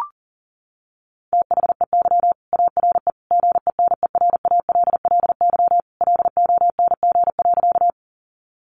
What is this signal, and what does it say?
Morse code, heard as a keyed tone, about 40 wpm, 700 Hz. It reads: THEY ARE GENERALLY LONG.
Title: Sentences from Top 600 Words 40wpm